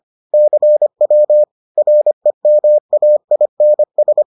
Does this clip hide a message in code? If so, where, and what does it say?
Yes: Morse code from 0.3 to 4.2 s, heard as a keyed tone, about 25 wpm, 600 Hz. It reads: CW REMAINS